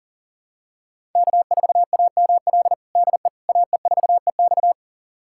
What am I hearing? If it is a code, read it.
Morse code, heard as a keyed tone, about 40 wpm, 700 Hz. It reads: K4AML DE AE4EX
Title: USA Callsign DE Callsign - 5 Characters - 40wpm